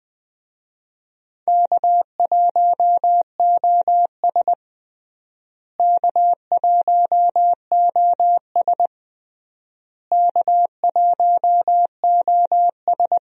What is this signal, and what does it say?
Morse code, heard as a keyed tone, about 20 wpm, 700 Hz. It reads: K1OS K1OS K1OS
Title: USA Call Signs - 4 Characters - Repeated 3x - 20wpm